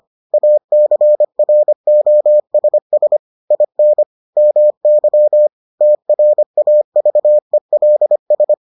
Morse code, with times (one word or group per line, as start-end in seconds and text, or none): 0.3-3.2 ACROSS
3.5-4.0 IN
4.4-5.5 MY
5.8-8.5 TRAVELS